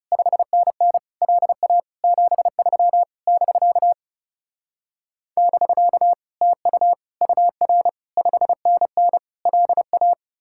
Morse code, heard as a keyed tone, about 36 wpm, 700 Hz. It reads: 5NN LA 73 <BK> <BK> TU UR 5NN LA
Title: Speed-Racing Common QSO Phrases - 30wpm